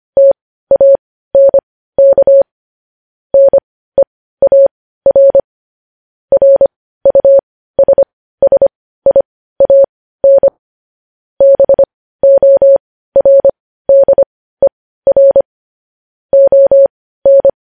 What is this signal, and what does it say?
Morse code, heard as a keyed tone, about 25 wpm, spaced as for 15 wpm, 550 Hz. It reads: TANK NEAR RUSSIAN BORDER ON
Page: News Headlines in Morse Code at 15 WPM